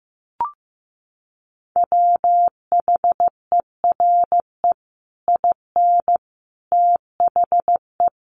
Morse code, heard as a keyed tone, about 15 wpm, 700 Hz. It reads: WHERE IN THE
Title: Sentences from Top 200 Words 15wpm